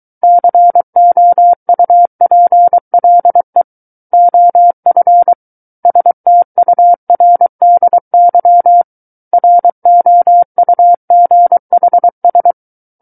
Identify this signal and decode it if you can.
Morse code, heard as a keyed tone, about 23 wpm, 700 Hz. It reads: COUPLE OF STURDY ROUGHS
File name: Book One / 14-Chapter 14 - 23 WPM